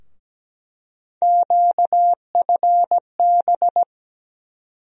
Morse code, heard as a keyed tone, about 17 wpm, 700 Hz. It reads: QFB
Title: USA Call Sign Postfixes - 3 Letters 17wpm